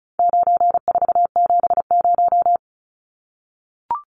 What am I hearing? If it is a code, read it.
Morse code, heard as a keyed tone, about 34 wpm, 700 Hz. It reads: 9470
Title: Numbers - 4-Digit 35wpm